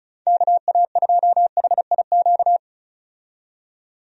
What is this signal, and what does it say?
Morse code, heard as a keyed tone, about 36 wpm, 700 Hz. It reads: KA2HIQ
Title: USA Call Signs - 6 Characters - 35wpm